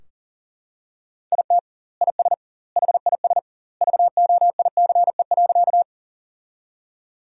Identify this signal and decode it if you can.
Morse code, heard as a keyed tone, about 40 wpm, 700 Hz. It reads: IT IS HIS VOICE.
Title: 2x Word-Spacing - Sentences from Top 500 Words - 40wpm